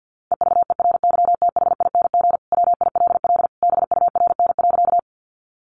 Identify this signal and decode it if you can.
Morse code, heard as a keyed tone, about 50 wpm, 700 Hz. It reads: EVERYTHING WILL BURN.
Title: Sentences from Top 1300 Words 50wpm